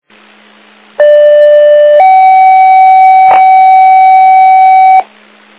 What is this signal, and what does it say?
Two-tone page, heard as a one-second tone, then a three-second tone, about 590 then 760 Hz, with noise. It is heard from 1.0 to 5.0 s.